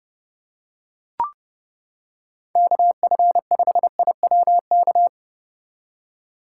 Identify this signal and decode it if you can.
Morse code, heard as a keyed tone, about 29 wpm, 700 Hz. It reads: KF5IWK